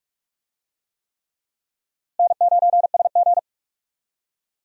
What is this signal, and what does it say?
Morse code, heard as a keyed tone, about 44 wpm, 700 Hz. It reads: N9SG
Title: USA Call Signs - 4 Characters - Repeated 3x - 45wpm